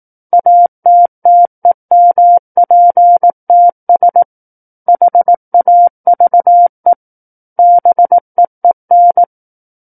Morse code, heard as a keyed tone, about 18 wpm, 700 Hz. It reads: ATTEMPTS HAVE BEEN
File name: Book One / 09-Chapter 9 - 18 WPM